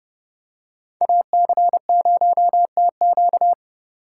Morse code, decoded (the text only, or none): AC0TQ